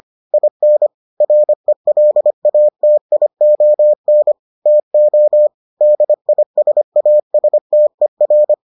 Morse code, heard as a keyed tone, about 25 wpm, 600 Hz. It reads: IN RELATION TO DISASTER